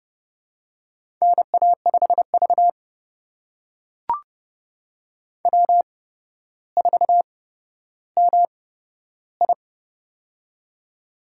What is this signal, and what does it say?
Morse code, heard as a keyed tone, about 30 wpm, spaced as for 13 wpm, 700 Hz. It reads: NA5V W4MI
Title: USA Call Signs - 4 Characters - 8x Character Spacing - 30wpm